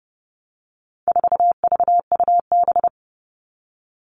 Morse code, heard as a keyed tone, about 30 wpm, 700 Hz. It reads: 4VUB
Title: All Cut Numbers - 4-Digit 30wpm